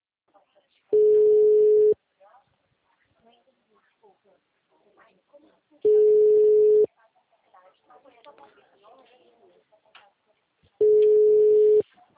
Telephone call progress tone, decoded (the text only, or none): ringback tone